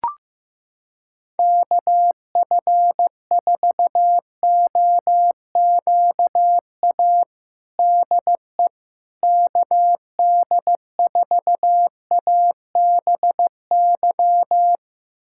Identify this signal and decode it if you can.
Morse code, heard as a keyed tone, about 15 wpm, 700 Hz. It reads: KF4OQA DE KD4ABY
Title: USA Callsign DE Callsign 15wpm